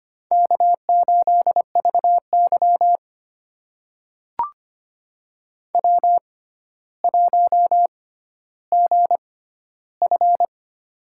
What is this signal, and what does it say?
Morse code, heard as a keyed tone, about 25 wpm, spaced as for 13 wpm, 700 Hz. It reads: K8VY W1GF
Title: USA Call Signs - 4 Characters - 6x Character Spacing - 25wpm